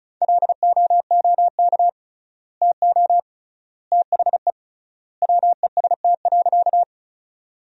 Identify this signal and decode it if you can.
Morse code, heard as a keyed tone, about 34 wpm, 700 Hz. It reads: LOOK TO THE WEST.
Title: 3x Word-Spacing - Sentences from Top 400 Words - 35wpm